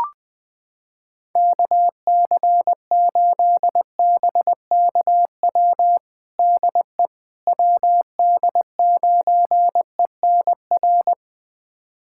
Morse code, heard as a keyed tone, about 21 wpm, 700 Hz. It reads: KC8BKW DE WD9ENR
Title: USA Callsign DE Callsign - 6 Characters - 20wpm